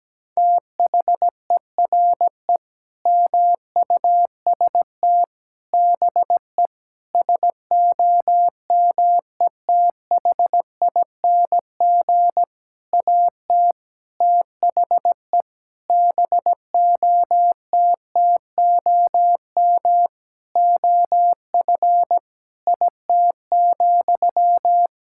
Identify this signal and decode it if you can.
Morse code, heard as a keyed tone, about 17 wpm, 700 Hz. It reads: THERE MUST BE SOMETHING AT THE BOTTOM OF IT,
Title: Sentences from Top 1000 Words - Review Part 6 - 17wpm